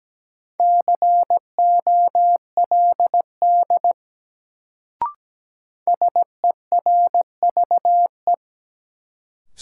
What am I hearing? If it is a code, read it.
Morse code, heard as a keyed tone, about 17 wpm, 700 Hz. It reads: COLD SERVE
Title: Sentences from Top 400 Words - New Word Drill - 17wpm